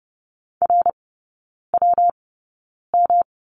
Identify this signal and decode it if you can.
Morse code, heard as a keyed tone, about 30 wpm, spaced as for 9 wpm, 700 Hz. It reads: RWM
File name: Lesson 19e - Letter w - Three Characters 30wpm